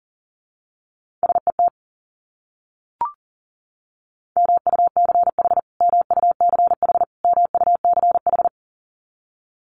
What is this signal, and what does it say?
Morse code, heard as a keyed tone, about 40 wpm, 700 Hz. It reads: SET MUCH MUCH MUCH
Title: Top 200 Words Repeated 40wpm